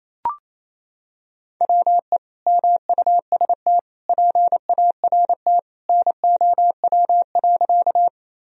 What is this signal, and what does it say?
Morse code, heard as a keyed tone, about 28 wpm, 700 Hz. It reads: WE MUST PART NOW.
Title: Sentences from Top 1000 Words - Review Part 1 - 28wpm